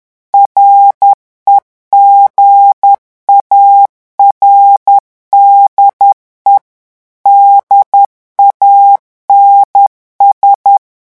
Morse code, heard as a keyed tone, about 10 wpm, 800 Hz. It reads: REGARDE DANS
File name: morse mp3 / revision francais 10 mots minute